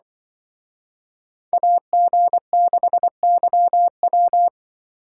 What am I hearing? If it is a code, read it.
Morse code, heard as a keyed tone, about 23 wpm, 700 Hz. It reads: AG6YW
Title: Speed Racing - USA 2x2 Call Signs 20wpm